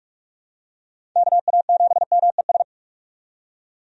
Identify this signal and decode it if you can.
Morse code, heard as a keyed tone, about 41 wpm, 700 Hz. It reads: KA7MES
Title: USA Call Signs - 6 Characters - 45wpm